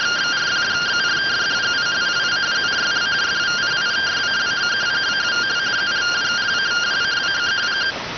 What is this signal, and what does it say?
Two-tone teleprinter signal, about 170 Hz shift.